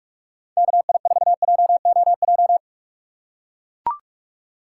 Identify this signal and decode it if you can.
Morse code, heard as a keyed tone, about 44 wpm, 700 Hz. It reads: KI4JOJ
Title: USA Call Signs - 6x Character Spacing - 45wpm